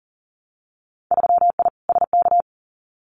Morse code, heard as a keyed tone, about 42 wpm, 700 Hz. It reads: 3I SK
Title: Winter Field Day Exchanges Encore 40wpm